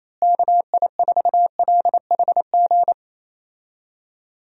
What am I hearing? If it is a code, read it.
Morse code, heard as a keyed tone, about 28 wpm, 700 Hz. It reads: KI4LHG
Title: USA Call Signs - 6 Characters - Repeated 2x - 28wpm